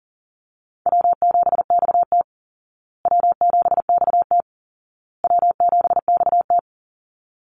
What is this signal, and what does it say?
Morse code, heard as a keyed tone, about 40 wpm, 700 Hz. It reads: W7XT W7XT W7XT
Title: Easy Call Signs Repeated 3x - 40wpm